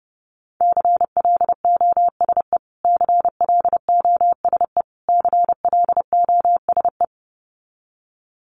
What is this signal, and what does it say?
Morse code, heard as a keyed tone, about 30 wpm, 700 Hz. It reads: CLOSE CLOSE CLOSE
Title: Top 500 Words Repeated 30wpm